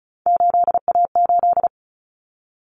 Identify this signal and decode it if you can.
Morse code, heard as a keyed tone, about 34 wpm, 700 Hz. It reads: 8A8
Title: Common Cut Numbers - 3-Digit 35wpm